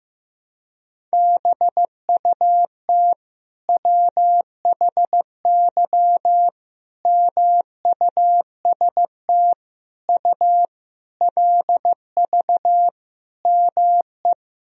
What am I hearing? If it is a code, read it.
Morse code, heard as a keyed tone, about 15 wpm, 700 Hz. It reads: BUT WHY MUST U LV ME
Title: Sentences from Top 400 Words and Abbreviations 15wpm